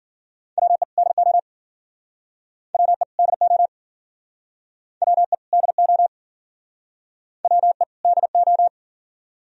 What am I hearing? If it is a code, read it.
Morse code, heard as a keyed tone, about 53 wpm, 700 Hz. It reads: WE DO WE DO WE DO WE DO